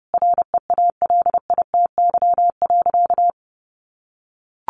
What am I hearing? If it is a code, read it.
Morse code, heard as a keyed tone, about 30 wpm, 700 Hz. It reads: REALITY.
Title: Sentences from Top 1200 Words 30wpm